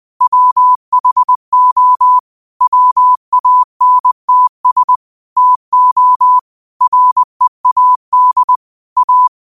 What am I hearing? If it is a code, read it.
Morse code, heard as a keyed tone, about 20 wpm, 1000 Hz. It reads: WHO WANTS TO READ A
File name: qotdfiles / 20wpm